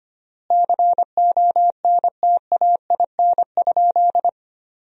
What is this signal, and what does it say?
Morse code, heard as a keyed tone, about 25 wpm, 700 Hz. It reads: CONTAIN?
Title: 3x Word-Spacing - Sentences from Top 500 Words - 25wpm